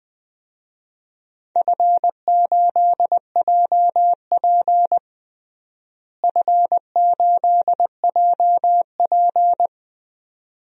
Morse code, heard as a keyed tone, about 20 wpm, 700 Hz. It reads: F8JP F8JP
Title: DXCC-100 Call Signs - 4 Characters - Repeated 2x - 20wpm